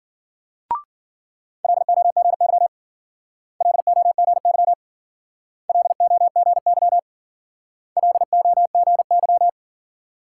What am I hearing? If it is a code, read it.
Morse code, heard as a keyed tone, about 52 wpm, 700 Hz. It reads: LOGY LOGY LOGY LOGY